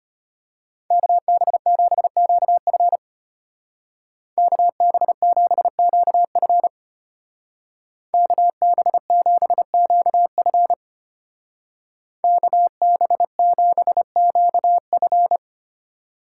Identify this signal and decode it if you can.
Morse code, heard as a keyed tone, about 30 wpm, 700 Hz. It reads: KB7QF KB7QF KB7QF KB7QF
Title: Speed Racing - USA Call Signs 25wpm